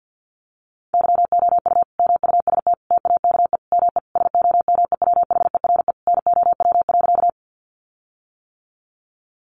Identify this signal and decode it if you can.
Morse code, heard as a keyed tone, about 50 wpm, 700 Hz. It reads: YOU MUST TAKE ME SOMEWHERE NOW.
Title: Sentences from Top 1400 Words 50wpm